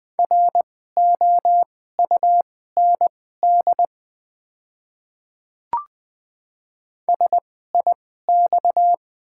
Morse code, heard as a keyed tone, about 20 wpm, spaced as for 14 wpm, 700 Hz. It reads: ROUND SIX
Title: 2x Character Spacing - Top 500 Words - 20wpm